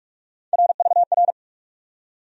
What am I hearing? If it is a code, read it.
Morse code, heard as a keyed tone, about 44 wpm, 700 Hz. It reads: RVR
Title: USA Call Sign Postfixes - 3 Letters 45wpm